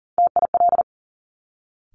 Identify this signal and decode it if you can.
Morse code, heard as a keyed tone, about 38 wpm, 700 Hz. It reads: TIL
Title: Top Three Letter Combinations in Words 40wpm